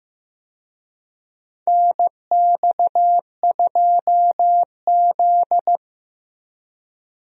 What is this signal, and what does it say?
Morse code, heard as a keyed tone, about 16 wpm, 700 Hz. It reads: NX2Z
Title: USA Call Signs - 4 Characters - Repeated 3x - 15wpm